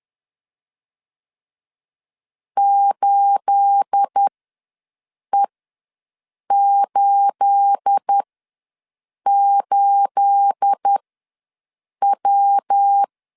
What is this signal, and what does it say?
Morse code, heard as a keyed tone, about 10 wpm, spaced as for 6 wpm, 800 Hz. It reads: 8E88W
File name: LEZIONI / 9 28 ws 10wpm 4124